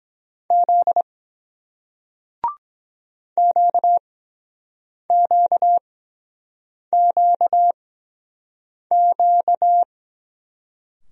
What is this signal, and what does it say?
Morse code, heard as a keyed tone, about 23 wpm, spaced as for 7 wpm, 700 Hz. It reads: Z QQQQ